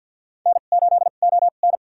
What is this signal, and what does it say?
Morse code, heard as a keyed tone, about 48 wpm, spaced as for 35 wpm, 700 Hz. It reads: N8ON
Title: USA Call Signs - 4 Characters - 2x Character Spacing - 50wpm